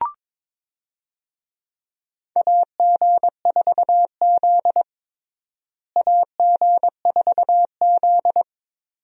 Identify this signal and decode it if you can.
Morse code, heard as a keyed tone, about 22 wpm, 700 Hz. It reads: AG4Z AG4Z
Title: USA Call Signs - 4 Characters - Repeated 2x - 22wpm